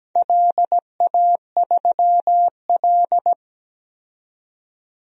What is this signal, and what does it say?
Morse code, heard as a keyed tone, about 17 wpm, 700 Hz. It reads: LA3L
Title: DXCC-100 Call Signs - 4 Characters - 17wpm